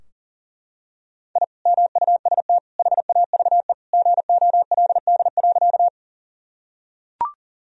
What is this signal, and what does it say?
Morse code, heard as a keyed tone, about 37 wpm, 700 Hz. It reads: I MUST HAVE GOLD.
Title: Sentences from Top 1000 Words - Review Part 3 - 40wpm